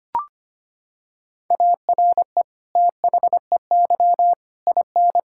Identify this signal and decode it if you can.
Morse code, heard as a keyed tone, about 25 wpm, 700 Hz. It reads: ARE THEY IN
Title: Sentences from Top 1000 Words - Review Part 5 - 25wpm